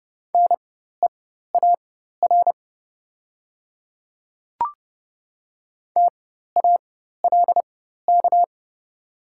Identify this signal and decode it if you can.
Morse code, heard as a keyed tone, about 31 wpm, spaced as for 14 wpm, 690 Hz. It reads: NEAR TALK
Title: 4x Character Spacing - Top 500 Words - 30wpm